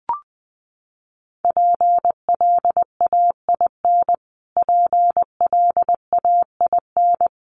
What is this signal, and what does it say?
Morse code, heard as a keyed tone, about 19 wpm, 700 Hz. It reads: PLAIN PLAIN
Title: Top 500 Words Repeated 20wpm